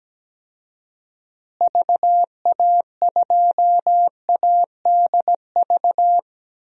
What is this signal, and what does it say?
Morse code, heard as a keyed tone, about 17 wpm, 700 Hz. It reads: VA2ADV
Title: DXCC-100 Call Signs - 6 Characters - Repeated 3x - 17wpm